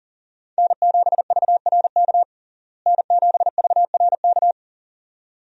Morse code, heard as a keyed tone, about 40 wpm, 700 Hz. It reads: N7VRK N7VRK
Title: USA Call Signs - 5 Characters - Repeated 2x - 40wpm